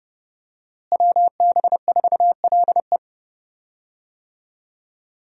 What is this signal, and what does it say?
Morse code, heard as a keyed tone, about 30 wpm, 700 Hz. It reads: WB4LE
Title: USA Call Signs - 5 Characters - Repeated 2x - 30wpm